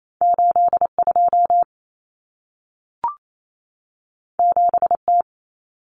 Morse code, heard as a keyed tone, about 28 wpm, 700 Hz. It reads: 82 7T
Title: Common Cut Numbers - 2-Digit 28wpm